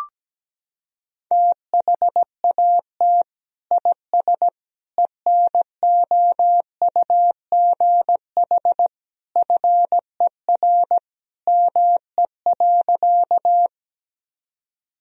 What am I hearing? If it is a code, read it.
Morse code, heard as a keyed tone, about 17 wpm, 700 Hz. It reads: THAT IS ENOUGH FER ME.
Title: Sentences from Top 400 Words and Abbreviations 17wpm